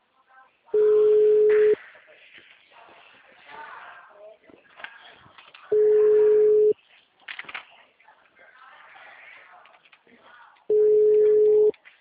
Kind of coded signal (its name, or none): telephone call progress tone